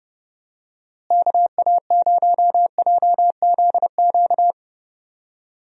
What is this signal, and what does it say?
Morse code, heard as a keyed tone, about 30 wpm, 700 Hz. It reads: KA0JZQ
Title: USA Call Signs - 6 Characters - Repeated 3x - 30wpm